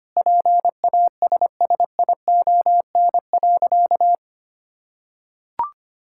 Morse code, heard as a keyed tone, about 25 wpm, 700 Hz. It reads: PASSION.
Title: Sentences from Top 1600 Words 25wpm